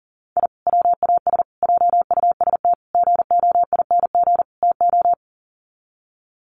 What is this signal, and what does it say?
Morse code, heard as a keyed tone, about 40 wpm, 700 Hz. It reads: I WAS JUST GOING TO
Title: Sets of 5 Words Encore 40wpm